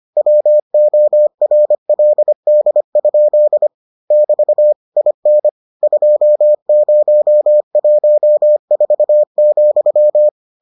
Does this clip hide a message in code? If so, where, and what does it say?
Yes: Morse code from 0.2 to 10.3 s, heard as a keyed tone, about 25 wpm, 600 Hz. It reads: WORLD? <BT> IN 2014,